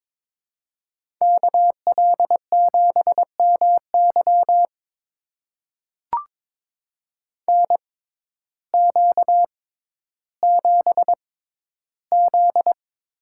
Morse code, heard as a keyed tone, about 22 wpm, spaced as for 12 wpm, 700 Hz. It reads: KL7MY NQ7Z
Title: USA Call Signs - 6x Character Spacing - 22wpm